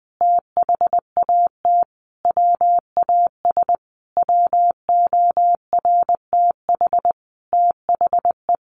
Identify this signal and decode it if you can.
Morse code, heard as a keyed tone, about 20 wpm, 700 Hz. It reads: THAT WAS WORTH THE
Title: Sentences from Top 1500 Words 20wpm